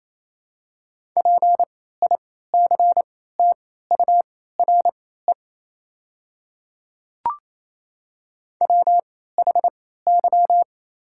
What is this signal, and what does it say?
Morse code, heard as a keyed tone, about 27 wpm, spaced as for 16 wpm, 700 Hz. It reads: PICTURE WHY
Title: 3x Character Spacing - Top 500 Words - 28wpm